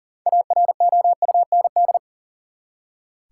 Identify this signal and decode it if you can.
Morse code, heard as a keyed tone, about 40 wpm, 700 Hz. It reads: AROUND
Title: Six Letter Words - 40wpm